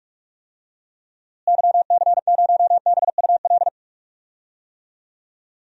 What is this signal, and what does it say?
Morse code, heard as a keyed tone, about 44 wpm, 700 Hz. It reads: YC0BUL
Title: DXCC-100 Call Signs - 6 Characters - Repeated 3x - 45wpm